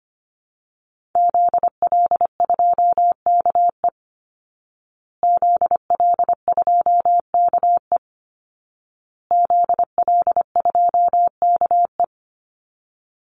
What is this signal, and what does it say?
Morse code, heard as a keyed tone, about 25 wpm, 700 Hz. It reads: ZL2KE ZL2KE ZL2KE
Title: Call Signs Repeated 2x - 25wpm